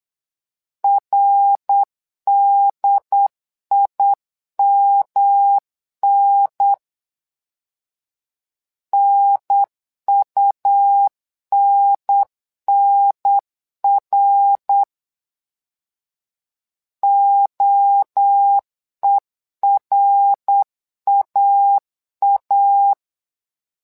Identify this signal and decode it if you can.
Morse code, heard as a keyed tone, about 8 wpm, 800 Hz. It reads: RDIMN NUNNR OERAA
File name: LEZIONI / 3 MWRO / 3 MWRO 8wpm 2606